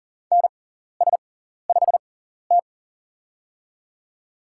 Morse code, heard as a keyed tone, about 40 wpm, spaced as for 14 wpm, 690 Hz. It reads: NS5T
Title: USA Call Signs - 4 Characters - 6x Character Spacing - 40wpm